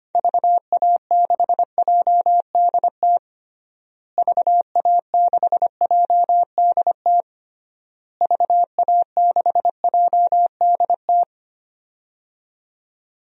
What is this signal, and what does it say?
Morse code, heard as a keyed tone, about 25 wpm, 700 Hz. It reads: VA6JDT VA6JDT VA6JDT
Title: DXCC-100 Call Signs - 6 Characters - Repeated 3x - 25wpm